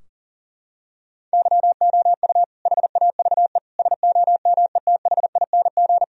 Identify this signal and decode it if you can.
Morse code, heard as a keyed tone, about 40 wpm, 700 Hz. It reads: YOU HAVE SOMETHING
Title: Sentences from Top 1000 Words - Review Part 4 - 40wpm